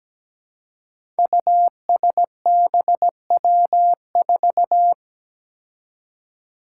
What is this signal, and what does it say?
Morse code, heard as a keyed tone, about 17 wpm, 700 Hz. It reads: USBW4
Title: Five Challenging Letter-Number Combinations 17wpm